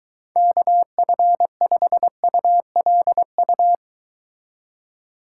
Morse code, heard as a keyed tone, about 23 wpm, 700 Hz. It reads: KF5ULU